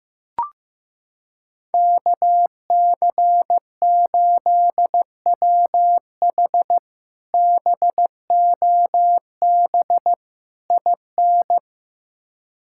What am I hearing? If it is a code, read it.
Morse code, heard as a keyed tone, about 16 wpm, 700 Hz. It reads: KC8WH BOB IN
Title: K1USN SST QSO Practice - 15wpm